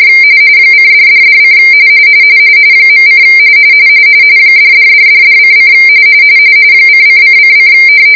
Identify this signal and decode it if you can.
RTTY, about 45.45 baud. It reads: :$$$: .: $$$$$:: :::::